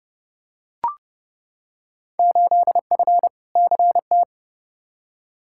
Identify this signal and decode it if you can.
Morse code, heard as a keyed tone, about 29 wpm, 700 Hz. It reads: 8F CT
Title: Field Day Exchanges Encore - 30wpm